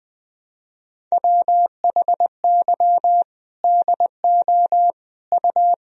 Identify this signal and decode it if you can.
Morse code, heard as a keyed tone, about 21 wpm, 700 Hz. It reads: WHY DO U
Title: Sentences from Top 400 Words and Abbreviations 20wpm